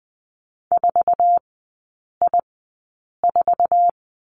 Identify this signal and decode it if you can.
Morse code, heard as a keyed tone, about 20 wpm, spaced as for 8 wpm, 700 Hz. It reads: 4I4
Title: 166 Lesson 27e - Number 7 - Three Characters 20wpm